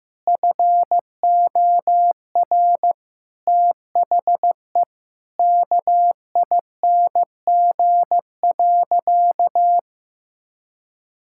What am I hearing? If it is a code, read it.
Morse code, heard as a keyed tone, about 15 wpm, 700 Hz. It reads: FOR THE KING.
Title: Sentences from Top 1000 Words - Review Part 6 - 15wpm